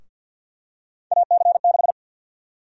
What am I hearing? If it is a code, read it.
Morse code, heard as a keyed tone, about 53 wpm, 700 Hz. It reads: AC6
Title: USA Call Sign Prefixes - 50wpm